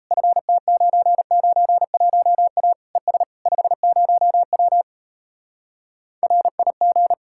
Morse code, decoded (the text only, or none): FT991A ES 50W RIG